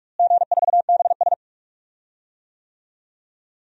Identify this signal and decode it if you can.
Morse code, heard as a keyed tone, about 48 wpm, 700 Hz. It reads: G4BS
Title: DXCC-100 Call Signs - 4 Characters - Repeated 3x - 45wpm